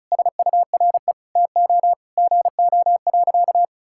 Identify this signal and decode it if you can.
Morse code, heard as a keyed tone, about 36 wpm, 700 Hz. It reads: SURE TO GO.
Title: Sentences from Top 1600 Words 35wpm